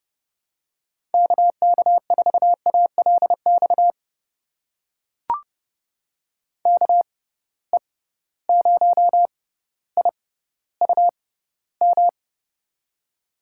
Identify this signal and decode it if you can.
Morse code, heard as a keyed tone, about 29 wpm, spaced as for 15 wpm, 700 Hz. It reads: KK4ALX KE0IUM
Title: USA Call Signs - 6 Characters - 6x Character Spacing - 30wpm